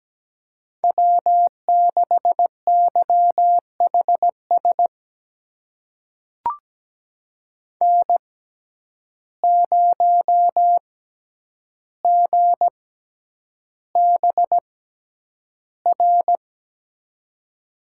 Morse code, heard as a keyed tone, about 17 wpm, spaced as for 9 wpm, 700 Hz. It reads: W6YHS N0GBR
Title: USA Call Signs - 5 Characters - 6x Character Spacing - 17wpm